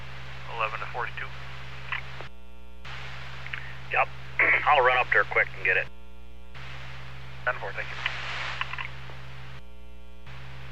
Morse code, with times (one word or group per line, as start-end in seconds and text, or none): none